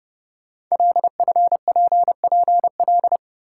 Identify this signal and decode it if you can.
Morse code, heard as a keyed tone, about 30 wpm, 700 Hz. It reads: LFPPL
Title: Code Groups - 5 Letters - 30wpm